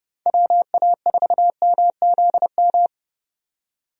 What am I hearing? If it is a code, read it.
Morse code, heard as a keyed tone, about 30 wpm, 700 Hz. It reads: WA4MZM
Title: USA Call Signs - 6 Characters - 30wpm